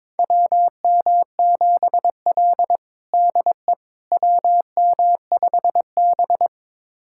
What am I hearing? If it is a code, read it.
Morse code, heard as a keyed tone, about 22 wpm, 700 Hz. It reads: WM7L DE WM5B
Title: USA Callsign DE Callsign - 4 Characters - 22wpm